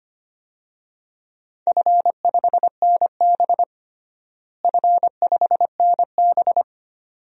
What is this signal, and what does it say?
Morse code, heard as a keyed tone, about 25 wpm, 700 Hz. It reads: F5NB F5NB
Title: DXCC-100 Call Signs - 4 Characters - Repeated 2x - 25wpm